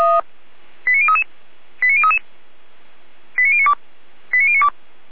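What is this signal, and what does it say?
Five-tone selective call (ZVEI), heard as quick runs of five tones, tones of about 70 ms. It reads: 89033 89033 89092 89092
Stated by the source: SDR-received ZVEI five-tone call, code 89085.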